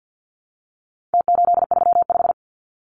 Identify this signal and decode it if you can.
Morse code, heard as a keyed tone, about 48 wpm, 700 Hz. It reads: T835
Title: Numbers - Contest Style - 4-Digit 50wpm